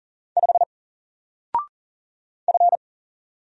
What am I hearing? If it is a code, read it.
Morse code, heard as a keyed tone, about 39 wpm, 700 Hz. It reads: 5 F
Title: Single Letter-Number Alt 40wpm